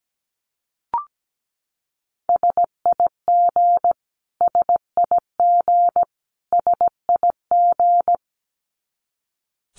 Morse code, heard as a keyed tone, about 17 wpm, 700 Hz. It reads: SIG SIG SIG
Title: Abbreviations Repeated 3x - 17wpm